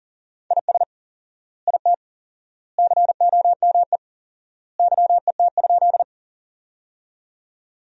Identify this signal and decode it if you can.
Morse code, heard as a keyed tone, about 40 wpm, 700 Hz. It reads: IS IT COME YET?
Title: 4x Word-Spacing - Sentences from Top 500 Words - 40wpm